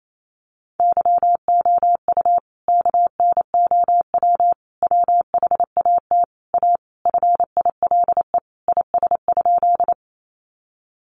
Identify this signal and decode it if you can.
Morse code, heard as a keyed tone, about 28 wpm, 700 Hz. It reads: YOU KNOW WHAT A FILE IS?
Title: Sentences from Top 1200 Words 28wpm